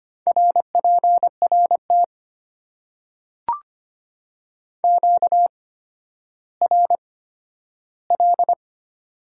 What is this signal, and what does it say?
Morse code, heard as a keyed tone, about 25 wpm, spaced as for 12 wpm, 700 Hz. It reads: RPRT QRL
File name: Top 90 QSO Elements - 8x Character Spacing - 25wpm